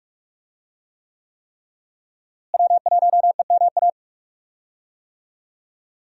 Morse code, heard as a keyed tone, about 46 wpm, 700 Hz. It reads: W1EMA
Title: USA Call Signs - 5 Characters - Repeated 3x - 45wpm